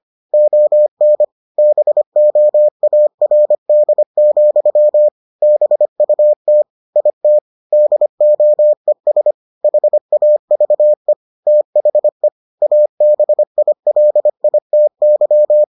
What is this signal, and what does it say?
Morse code, heard as a keyed tone, about 25 wpm, 600 Hz. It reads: ON BOARD, BUT IT DOES HAVE THE ABILITY